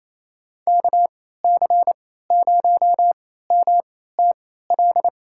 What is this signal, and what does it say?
Morse code, heard as a keyed tone, about 28 wpm, spaced as for 16 wpm, 700 Hz. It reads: KC0MTL
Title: USA Call Signs - 3x Character Spacing - 28wpm